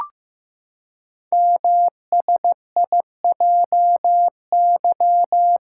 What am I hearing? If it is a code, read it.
Morse code, heard as a keyed tone, about 15 wpm, 700 Hz. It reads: MSIJY
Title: Code Groups - 5 Letters - 15wpm